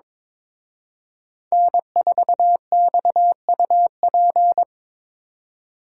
Morse code, heard as a keyed tone, about 26 wpm, 700 Hz. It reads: N4XUP
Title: Speed Racing - USA 1x3 Call Signs 22wpm